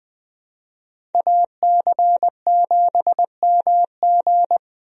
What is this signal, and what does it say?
Morse code, heard as a keyed tone, about 20 wpm, 700 Hz. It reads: AC7MG
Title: USA Call Signs - 5 Characters - 20wpm